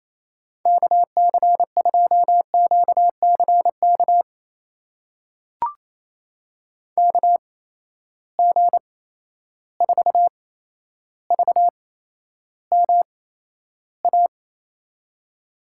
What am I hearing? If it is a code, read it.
Morse code, heard as a keyed tone, about 28 wpm, spaced as for 12 wpm, 700 Hz. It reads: KC2QCK KG4VMA
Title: USA Call Signs - 6 Characters - 8x Character Spacing - 28wpm